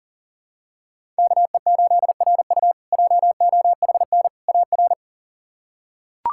Morse code, heard as a keyed tone, about 40 wpm, 690 Hz. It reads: KE8RU JOHN AR
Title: K1USN SST QSO Practice - 40wpm